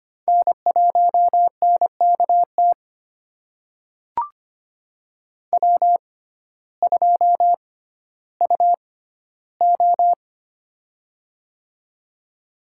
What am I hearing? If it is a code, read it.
Morse code, heard as a keyed tone, about 25 wpm, spaced as for 14 wpm, 700 Hz. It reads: N1NKT W2UO
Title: USA Call Signs - 6x Character Spacing - 25wpm